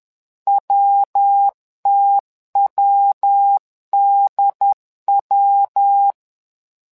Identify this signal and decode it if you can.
Morse code, heard as a keyed tone, about 10 wpm, 800 Hz. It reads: WTWDW
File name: LEZIONI / 3 MWRO / 3 MWRO 10wpm 2148